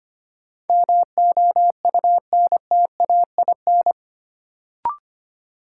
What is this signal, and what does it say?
Morse code, heard as a keyed tone, about 25 wpm, 700 Hz. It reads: MOUNTAIN
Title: Sentences from Top 1300 Words - New Word Drill Alt - 25wpm